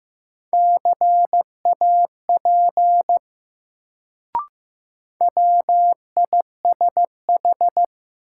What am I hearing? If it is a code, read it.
Morse code, heard as a keyed tone, about 15 wpm, 700 Hz. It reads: CAP WISH
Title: Sentences from Top 1600 Words - New Word Drill Alt - 15wpm